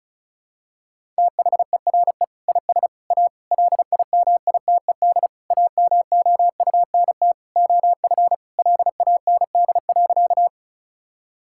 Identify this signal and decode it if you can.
Morse code, heard as a keyed tone, about 34 wpm, 700 Hz. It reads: THERE IS A LIMITED AMOUNT OF LAND.
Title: Sentences from Top 1000 Words - Review Part 6 - 35wpm